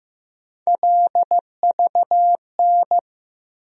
Morse code, heard as a keyed tone, about 15 wpm, 700 Hz. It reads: LVN
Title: USA Call Sign Postfixes - 3 Letters 15wpm